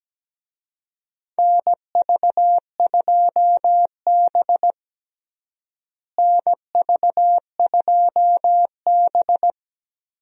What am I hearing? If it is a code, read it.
Morse code, heard as a keyed tone, about 17 wpm, 700 Hz. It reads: NV2B NV2B
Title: USA Call Signs - 4 Characters - Repeated 2x - 17wpm